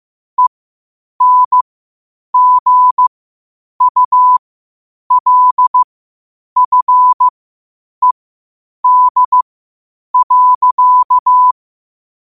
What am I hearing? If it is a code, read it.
Morse code, heard as a keyed tone, about 15 wpm, spaced as for 8 wpm, 1000 Hz. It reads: ENGULFED.
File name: qotdfiles / 8wpm